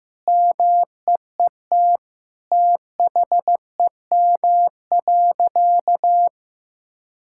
Morse code, heard as a keyed tone, about 15 wpm, 700 Hz. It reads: MEET THEM.
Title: Sentences from Top 1000 Words - Review Part 6 - 15wpm